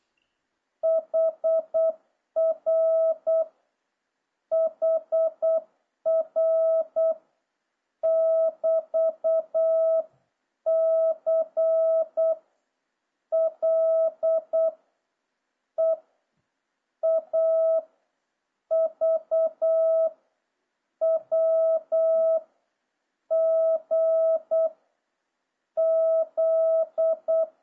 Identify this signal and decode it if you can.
Morse code, heard as a keyed tone, about 8 wpm, 650 Hz. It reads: HR HR <BT>C L E A V W G Z